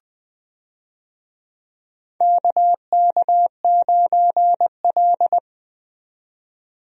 Morse code, heard as a keyed tone, about 20 wpm, 700 Hz. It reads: KK9L
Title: USA Call Signs - 4 Characters - 20wpm